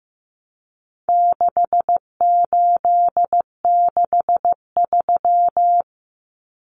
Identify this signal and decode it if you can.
Morse code, heard as a keyed tone, about 15 wpm, 700 Hz. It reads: 6863
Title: Common Cut Numbers - 4-Digit 15wpm